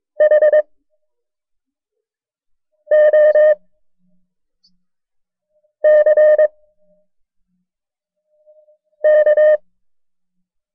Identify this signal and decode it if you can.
Morse code, heard as a keyed tone, about 22 wpm, 610 Hz. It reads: H O C K
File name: FreeFiles / CW-practice / KB6MT-tape1-side2